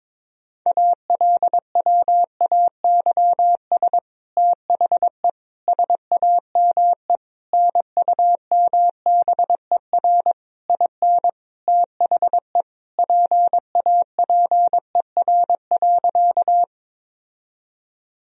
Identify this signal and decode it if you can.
Morse code, heard as a keyed tone, about 22 wpm, 700 Hz. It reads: ALWAYS THE SAME NUMBER IN THE PAPER.
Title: Sentences from Top 1000 Words - Review Part 1 - 22wpm